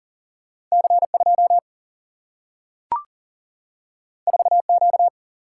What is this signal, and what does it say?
Morse code, heard as a keyed tone, about 40 wpm, 700 Hz. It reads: C 2 4 Q
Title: Single-Letter Single-Number 40wpm